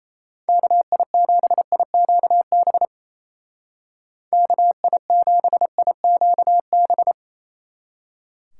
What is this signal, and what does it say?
Morse code, heard as a keyed tone, about 34 wpm, 700 Hz. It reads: KI7IQB KI7IQB
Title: Speed Racing - USA 2x3 Call Signs 28wpm